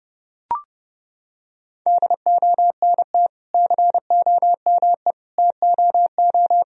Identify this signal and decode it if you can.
Morse code, heard as a keyed tone, about 32 wpm, 700 Hz. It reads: DONT COME TOO
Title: Sentences from Top 1000 Words - Review Part 1 - 30wpm